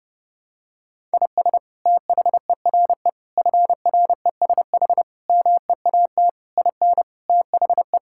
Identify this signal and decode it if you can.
Morse code, heard as a keyed tone, about 32 wpm, 700 Hz. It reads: IS THERE FRESH MEAT IN THE
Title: Sentences from Top 1600 Words 30wpm